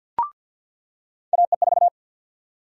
Morse code, heard as a keyed tone, about 46 wpm, 700 Hz. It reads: AE4
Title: USA Call Sign Prefixes - 50wpm